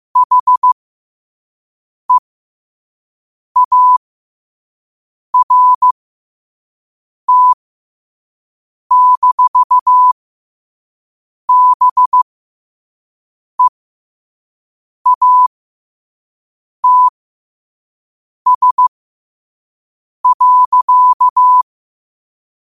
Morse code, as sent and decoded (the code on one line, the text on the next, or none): .... .- .-. - -....- -... .- - ... .-.-.-
HEART-BEATS.